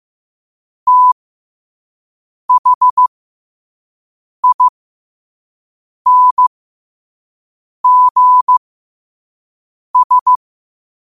Morse code, heard as a keyed tone, about 15 wpm, spaced as for 5 wpm, 1000 Hz. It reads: THINGS
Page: New quotes every day in morse code at 5 Words per minute.